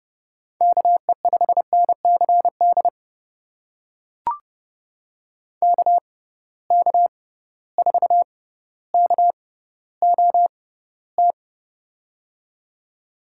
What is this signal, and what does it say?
Morse code, heard as a keyed tone, about 30 wpm, spaced as for 15 wpm, 700 Hz. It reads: KE5NCD KK4KOT
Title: USA Call Signs - 6 Characters - 6x Character Spacing - 30wpm